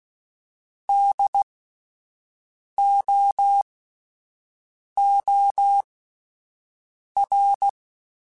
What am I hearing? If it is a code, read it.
Morse code, heard as a keyed tone, about 16 wpm, spaced as for 5 wpm, 800 Hz. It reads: DOOR